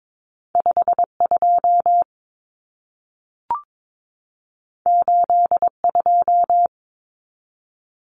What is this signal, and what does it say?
Morse code, heard as a keyed tone, about 22 wpm, 700 Hz. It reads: 52 82
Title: Common Cut Numbers - 2-Digit 22wpm